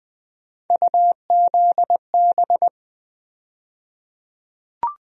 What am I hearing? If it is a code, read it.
Morse code, heard as a keyed tone, about 20 wpm, 700 Hz. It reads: UZB
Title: USA Call Sign Postfixes - 3 Letters 20wpm